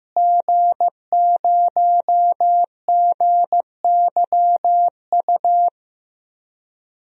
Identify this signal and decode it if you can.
Morse code, heard as a keyed tone, about 15 wpm, 700 Hz. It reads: G0GYU
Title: DXCC-100 Call Signs - 5 Characters - Repeated 3x - 15wpm